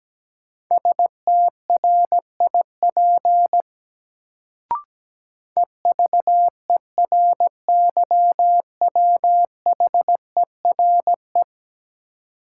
Morse code, heard as a keyed tone, about 17 wpm, 700 Hz. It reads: STRIP EVERYWHERE
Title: Sentences from Top 1600 Words - New Word Drill - 17wpm